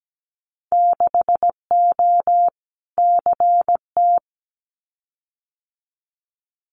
Morse code, heard as a keyed tone, about 17 wpm, 700 Hz. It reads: 6O CT
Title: Winter Field Day Exchanges Encore 17wpm